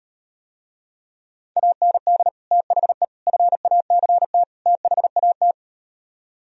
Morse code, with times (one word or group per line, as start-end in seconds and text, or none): 1.6-2.3 AND
2.5-3.0 THE
3.3-4.4 FACT
4.7-5.5 THAT